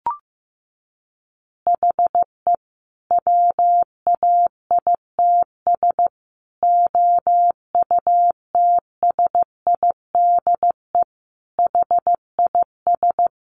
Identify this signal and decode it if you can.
Morse code, heard as a keyed tone, about 15 wpm, 700 Hz. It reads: HE WAITS OUTSIDE HIS
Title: Additional Easy Phrases 15wpm